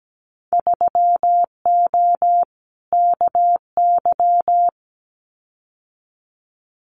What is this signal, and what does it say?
Morse code, heard as a keyed tone, about 17 wpm, 700 Hz. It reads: 3O KY
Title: Winter Field Day Exchanges 17wpm